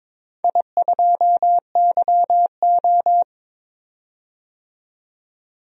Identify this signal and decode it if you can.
Morse code, heard as a keyed tone, about 22 wpm, 700 Hz. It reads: I2YO
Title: DXCC-100 Call Signs - 4 Characters - 22wpm